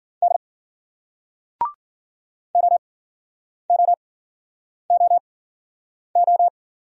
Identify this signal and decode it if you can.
Morse code, heard as a keyed tone, about 50 wpm, spaced as for 9 wpm, 700 Hz. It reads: D OOOO